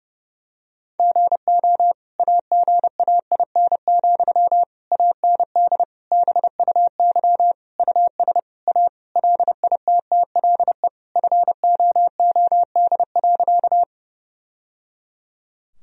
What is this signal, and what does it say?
Morse code, heard as a keyed tone, about 32 wpm, 700 Hz. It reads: GO AGAIN, AND BUY US A LITTLE FOOD.
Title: Sentences from Top 1000 Words - Review Part 4 - 30wpm